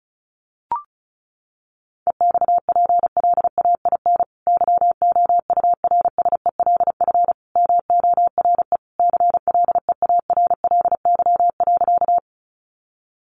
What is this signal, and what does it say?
Morse code, heard as a keyed tone, about 34 wpm, 700 Hz. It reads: EXPLAIN YOURSELF MORE CLEARLY.
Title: Sentences from Top 1300 Words 35wpm